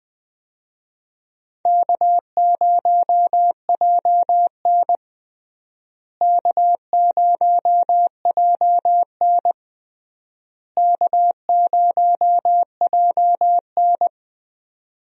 Morse code, heard as a keyed tone, about 20 wpm, 700 Hz. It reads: K0JN K0JN K0JN
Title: USA Call Signs - 4 Characters - Repeated 3x - 20wpm